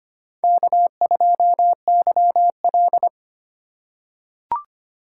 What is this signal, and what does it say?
Morse code, heard as a keyed tone, about 25 wpm, 700 Hz. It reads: K2YL
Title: USA Call Signs - 4 Characters - 3x Character Spacing - 25wpm